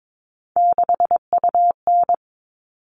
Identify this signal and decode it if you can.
Morse code, heard as a keyed tone, about 22 wpm, 700 Hz. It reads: 6UN
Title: All Cut Numbers - 3-Digit 22wpm